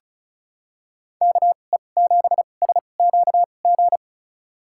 Morse code, heard as a keyed tone, about 36 wpm, spaced as for 25 wpm, 700 Hz. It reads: KE7SQG
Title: USA Call Signs - 6 Characters - 2x Character Spacing - 35wpm